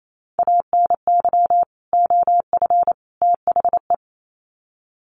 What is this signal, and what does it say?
Morse code, heard as a keyed tone, about 28 wpm, 700 Hz. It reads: ANY OF THE
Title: Sets of 3 Words 28wpm